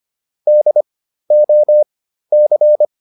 Morse code, heard as a keyed tone, about 25 wpm, spaced as for 13 wpm, 600 Hz. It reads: DOC
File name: PR202 13